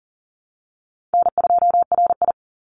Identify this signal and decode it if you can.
Morse code, heard as a keyed tone, about 38 wpm, 700 Hz. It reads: N2RI
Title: Call Signs - 40wpm